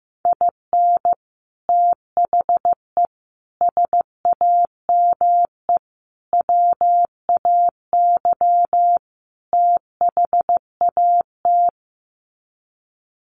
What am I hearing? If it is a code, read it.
Morse code, heard as a keyed tone, about 15 wpm, 700 Hz. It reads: IN THE SAME WAY THAT
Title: Sets of 5 Words 15wpm